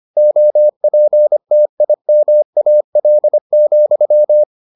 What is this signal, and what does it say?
Morse code, heard as a keyed tone, about 25 wpm, 600 Hz. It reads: OPTIMAL,